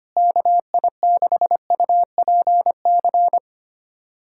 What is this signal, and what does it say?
Morse code, heard as a keyed tone, about 25 wpm, 700 Hz. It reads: KI6UPC
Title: USA Callsign DE Callsign 25wpm